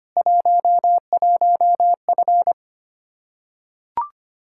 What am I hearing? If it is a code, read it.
Morse code, heard as a keyed tone, about 25 wpm, 700 Hz. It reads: 11F
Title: Practice QSO - Exchange 2 - Info - 25wpm